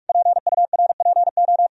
Morse code, heard as a keyed tone, about 41 wpm, 700 Hz. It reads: PURPO
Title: Sentences from Top 1000 Words - Review Part 2 - 45wpm